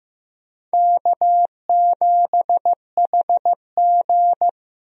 Morse code, heard as a keyed tone, about 15 wpm, 700 Hz. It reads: K7HG
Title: USA Call Signs - 4 Characters - Repeated 2x - 15wpm